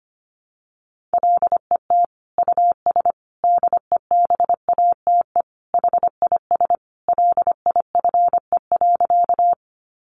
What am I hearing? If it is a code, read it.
Morse code, heard as a keyed tone, about 25 wpm, 700 Hz. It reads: LET US DEBATE HIS LIFE.
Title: Sentences from Top 1100 Words 25wpm